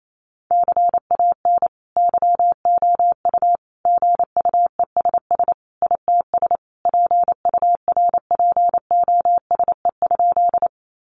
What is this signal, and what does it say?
Morse code, heard as a keyed tone, about 28 wpm, 700 Hz. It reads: CAN YOU GUESS ITS PURPOSE?
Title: Sentences from Top 1500 Words 28wpm